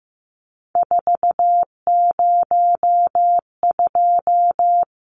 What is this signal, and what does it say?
Morse code, heard as a keyed tone, about 15 wpm, 700 Hz. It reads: 402
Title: Numbers - 3-Digit 15wpm